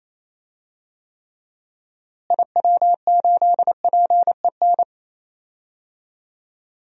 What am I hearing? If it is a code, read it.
Morse code, heard as a keyed tone, about 28 wpm, 700 Hz. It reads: IW8PEN
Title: DXCC-100 Call Signs - 6 Characters - Repeated 2x - 28wpm